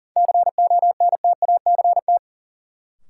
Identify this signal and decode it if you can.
Morse code, heard as a keyed tone, about 37 wpm, 700 Hz. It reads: CONTACT
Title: Sentences from Top 1000 Words - Review Part 7 - New Word Drill - 40wpm